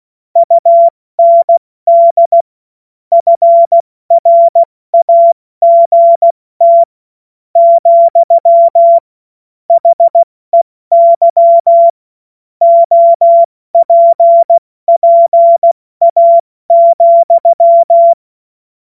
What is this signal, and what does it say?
Morse code, heard as a keyed tone, about 16 wpm, 680 Hz. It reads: UND FRAGT , HEY OPPA,
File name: Morsekurs / Witze / Witz 23. Dezember 2025 70 F80